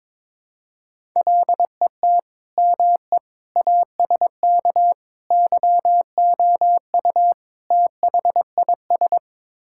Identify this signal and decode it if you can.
Morse code, heard as a keyed tone, about 22 wpm, 700 Hz. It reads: LET ME ASK YOU THIS
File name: practice / 22wpm / Build A Phrase - 5-Word Phrases - 22wpm